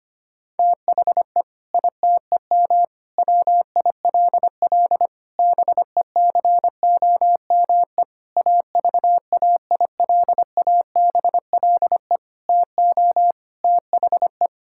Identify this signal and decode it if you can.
Morse code, heard as a keyed tone, about 25 wpm, 700 Hz. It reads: THE ITEM WILL BECOME AVAILABLE TO THE
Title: Sentences from Top 1000 Words - Review Part 6 - 25wpm